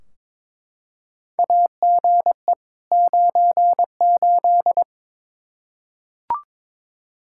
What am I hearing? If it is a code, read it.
Morse code, heard as a keyed tone, about 22 wpm, 700 Hz. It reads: AGE 98
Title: Practice QSO - Exchange 3 - More Info - 22wpm